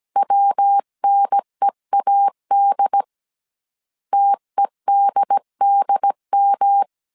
Morse code, heard as a keyed tone, about 17 wpm, 800 Hz. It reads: WNEAD TEDDM
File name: CW / DET / 1 3 1500C S C 16wpm 2148